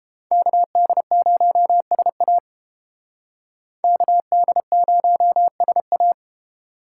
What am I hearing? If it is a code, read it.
Morse code, heard as a keyed tone, about 30 wpm, 700 Hz. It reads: KD0SA KD0SA